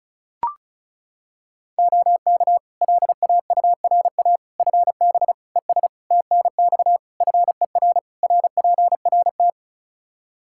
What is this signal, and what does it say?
Morse code, heard as a keyed tone, about 36 wpm, 700 Hz. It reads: OK LAURA FB ES TNX FER RPRT
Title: K1USN SST QSO Practice 35wpm